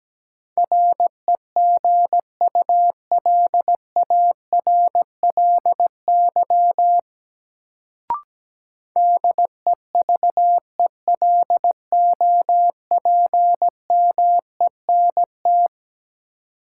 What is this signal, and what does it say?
Morse code, heard as a keyed tone, about 17 wpm, 700 Hz. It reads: REGULARLY DEVELOPMENT
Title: Sentences from Top 1400 Words - New Word Drill Alt - 17wpm